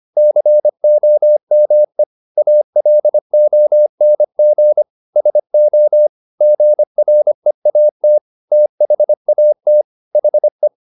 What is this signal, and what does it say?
Morse code, heard as a keyed tone, about 25 wpm, 600 Hz. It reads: COME ALONG SO GREAT THAT HE
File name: the old man and the sea part 1 25wpm